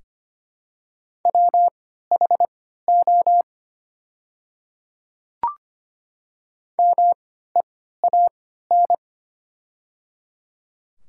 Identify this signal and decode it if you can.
Morse code, heard as a keyed tone, about 25 wpm, spaced as for 14 wpm, 700 Hz. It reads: WHO MEAN
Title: 3x Character Spacing - Top 200 Words - 25wpm